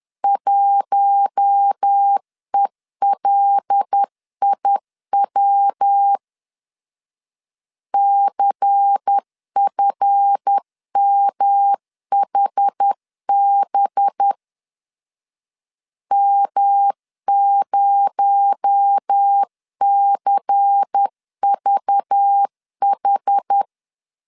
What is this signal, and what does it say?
Morse code, heard as a keyed tone, about 10 wpm, 800 Hz. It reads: 1ELIW CFMHB M0CVH
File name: LEZIONI / 7 GF9Z / 7 GF 10wpm 2514